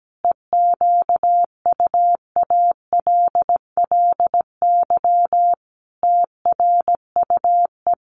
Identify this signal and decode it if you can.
Morse code, heard as a keyed tone, about 17 wpm, 700 Hz. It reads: EQUALLY TRUE
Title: Sentences from Top 1400 Words 17wpm